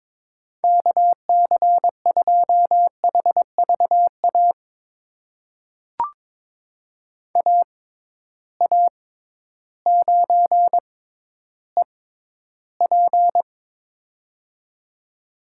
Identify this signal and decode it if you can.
Morse code, heard as a keyed tone, about 22 wpm, spaced as for 12 wpm, 700 Hz. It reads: KC2HVA AA9EP
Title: USA Call Signs - 6x Character Spacing - 22wpm